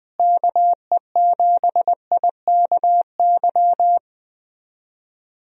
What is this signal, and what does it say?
Morse code, heard as a keyed tone, about 20 wpm, 700 Hz. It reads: KE7IKY
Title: USA Call Signs - 6 Characters - Repeated 2x - 20wpm